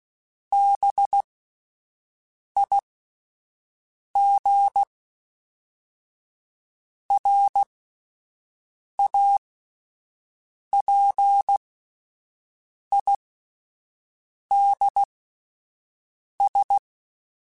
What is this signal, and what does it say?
Morse code, heard as a keyed tone, about 16 wpm, spaced as for 5 wpm, 800 Hz. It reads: BIGRAPIDS
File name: fromSMs / FromSMs mp3s / Cities@5